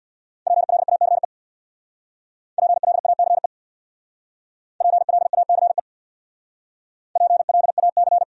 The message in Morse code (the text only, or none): PLACE PLACE PLACE PLAC